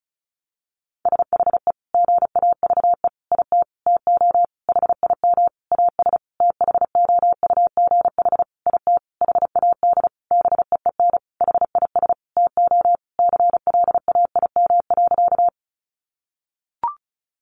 Morse code, heard as a keyed tone, about 34 wpm, 700 Hz. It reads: SHE GAVE IT TO HIM AS THOUGH IT HAD BEEN HIS TO CLAIM.
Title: Sentences from Top 1500 Words 35wpm